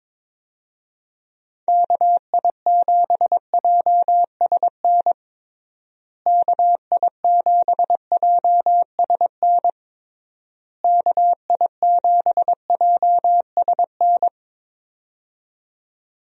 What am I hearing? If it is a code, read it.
Morse code, heard as a keyed tone, about 22 wpm, 700 Hz. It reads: KI7JSN KI7JSN KI7JSN
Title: USA Call Signs - 6 Characters - Repeated 3x - 22wpm